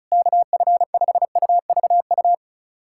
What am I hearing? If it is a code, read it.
Morse code, heard as a keyed tone, about 34 wpm, 700 Hz. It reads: KF5UVU
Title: USA Callsign DE Callsign - 6 Characters - 35wpm